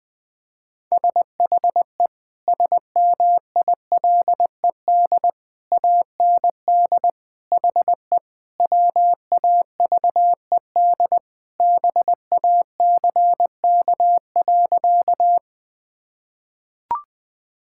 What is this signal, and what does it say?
Morse code, heard as a keyed tone, about 21 wpm, 700 Hz. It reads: SHE SMILED AND HE WAVED BACK.
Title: Sentences with 2-Word Phrases - 20wpm